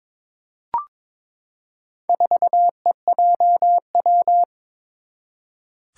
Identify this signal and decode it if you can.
Morse code, heard as a keyed tone, about 23 wpm, 700 Hz. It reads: 4EJW
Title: Four Challenging Letter-Number Combinations 22wpm